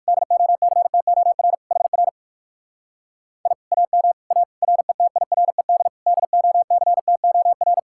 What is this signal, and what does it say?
Morse code, heard as a keyed tone, about 59 wpm, 700 Hz. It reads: DOCTOR HR I AM A RETIRED DOCTOR